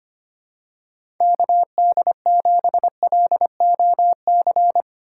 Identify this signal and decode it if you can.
Morse code, heard as a keyed tone, about 25 wpm, 700 Hz. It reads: KD7LOC
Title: USA Call Signs - 6 Characters - 4x Character Spacing - 25wpm